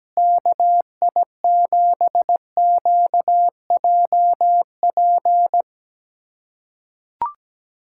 Morse code, heard as a keyed tone, about 17 wpm, 700 Hz. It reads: KI7QJP
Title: USA Callsign DE Callsign - 6 Characters - 17wpm